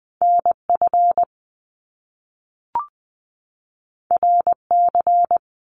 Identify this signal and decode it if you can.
Morse code, heard as a keyed tone, about 20 wpm, 700 Hz. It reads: NF RC
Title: Top Two Letter Combinations in Words 20wpm